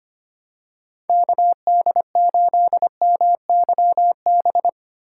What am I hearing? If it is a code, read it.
Morse code, heard as a keyed tone, about 25 wpm, 700 Hz. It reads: KD8MYB
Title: USA Call Signs - 2x Character Spacing - 25wpm